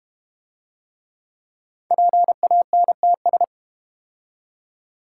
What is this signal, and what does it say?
Morse code, heard as a keyed tone, about 32 wpm, 700 Hz. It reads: PANTS